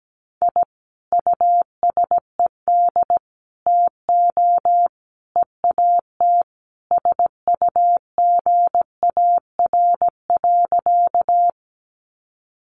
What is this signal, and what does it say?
Morse code, heard as a keyed tone, about 17 wpm, 700 Hz. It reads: I USED TO EAT SUGAR.
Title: Sentences from Top 1200 Words 17wpm